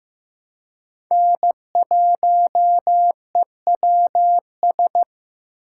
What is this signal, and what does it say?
Morse code, heard as a keyed tone, about 15 wpm, 700 Hz. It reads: N1EWS
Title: USA Call Signs - 5 Characters - 4x Character Spacing - 15wpm